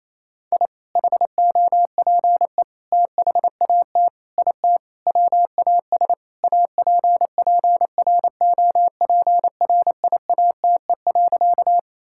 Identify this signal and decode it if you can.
Morse code, heard as a keyed tone, about 28 wpm, 700 Hz. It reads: I HOPE THAT IT WAS APPROPRIATE.
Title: Sentences from Top 1000 Words - Review Part 5 - 28wpm